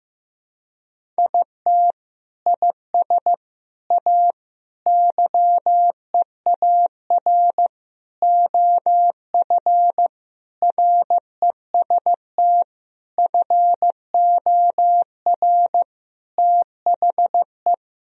Morse code, heard as a keyed tone, about 15 wpm, 700 Hz. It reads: IT IS A YEAR OF REST FOR THE
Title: Sentences from Top 1000 Words - Review Part 1 - 15wpm